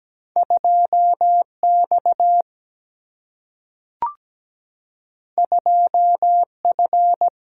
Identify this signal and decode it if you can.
Morse code, heard as a keyed tone, about 17 wpm, 700 Hz. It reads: 2X 2F
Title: Single-Letter Single-Number 17wpm